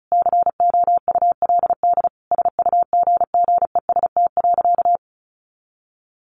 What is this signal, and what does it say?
Morse code, heard as a keyed tone, about 34 wpm, 700 Hz. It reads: COULD SUGGEST.
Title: Sentences from Top 1500 Words 35wpm